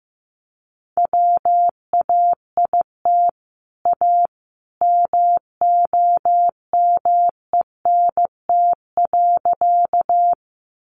Morse code, heard as a keyed tone, about 15 wpm, 700 Hz. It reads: WAIT A MOMENT.
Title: Sentences from Top 1000 Words 15wpm